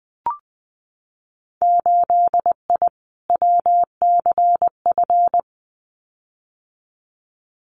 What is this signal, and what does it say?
Morse code, heard as a keyed tone, about 20 wpm, 700 Hz. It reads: 8I WCF
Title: Winter Field Day Exchanges Encore 20wpm